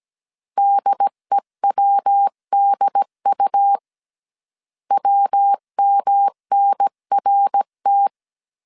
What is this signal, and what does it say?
Morse code, heard as a keyed tone, about 17 wpm, 800 Hz. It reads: DEWDU WMNRT
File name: CW / DET / 1 3 1500C S A 16wpm 2141